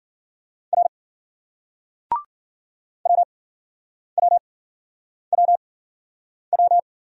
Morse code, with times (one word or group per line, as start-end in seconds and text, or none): 0.7-0.9 R
3.1-6.8 WWWW